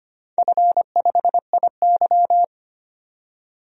Morse code, heard as a keyed tone, about 25 wpm, 700 Hz. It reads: F5IY
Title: DXCC-100 Call Signs - 4 Characters - 25wpm